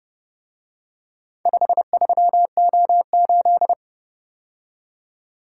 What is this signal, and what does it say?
Morse code, heard as a keyed tone, about 29 wpm, 700 Hz. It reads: 53O8